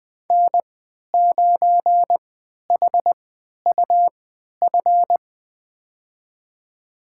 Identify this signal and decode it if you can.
Morse code, heard as a keyed tone, about 20 wpm, spaced as for 11 wpm, 700 Hz. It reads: N9HUF
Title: USA Call Signs - 3x Character Spacing - 20wpm